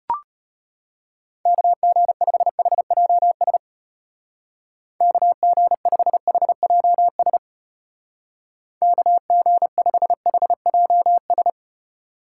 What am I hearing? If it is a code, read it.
Morse code, heard as a keyed tone, about 31 wpm, 700 Hz. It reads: KG5HJS KG5HJS KG5HJS